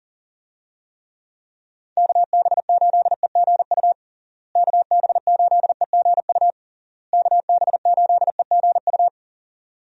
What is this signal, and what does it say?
Morse code, heard as a keyed tone, about 40 wpm, 700 Hz. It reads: KB8EGU KB8EGU KB8EGU
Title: USA Call Signs - 6 Characters - Repeated 3x - 40wpm